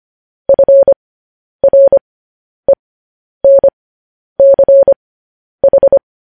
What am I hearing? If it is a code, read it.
Morse code, heard as a keyed tone, about 25 wpm, spaced as for 10 wpm, 550 Hz. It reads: FRENCH
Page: news-headlines-in-morse-code-at-10-wpm